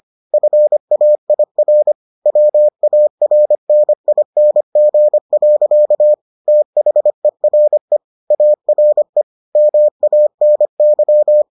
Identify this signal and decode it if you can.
Morse code, heard as a keyed tone, about 25 wpm, 600 Hz. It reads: FAIR WARNING. THERE ARE MANY